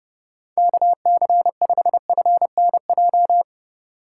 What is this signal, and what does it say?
Morse code, heard as a keyed tone, about 30 wpm, 700 Hz. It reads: KC5FNJ
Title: USA Call Signs - 6 Characters - 3x Character Spacing - 30wpm